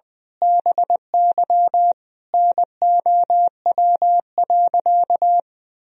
Morse code, heard as a keyed tone, about 20 wpm, 700 Hz. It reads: BY NOW.
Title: Sentences from Top 1000 Words - Review Part 3 - 20wpm